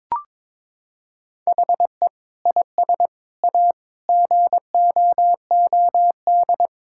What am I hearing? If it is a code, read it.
Morse code, heard as a keyed tone, about 22 wpm, 700 Hz. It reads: HE IS A GOOD
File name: practice / 22wpm / Sets of 8 Words - 22wpm